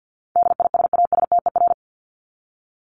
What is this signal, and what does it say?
Morse code, heard as a keyed tone, about 50 wpm, 700 Hz. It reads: DISASTER
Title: Top 1501-1750 Words 50wpm